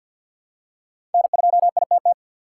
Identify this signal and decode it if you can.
Morse code, heard as a keyed tone, about 51 wpm, 700 Hz. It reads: N2ITT
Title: USA Call Signs - 5 Characters - 3x Character Spacing - 50wpm